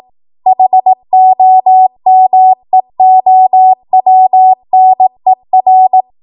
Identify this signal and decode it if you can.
Morse code, heard as a keyed tone, about 18 wpm, 750 Hz. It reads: HOMEOWNER
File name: Morse / 18WPM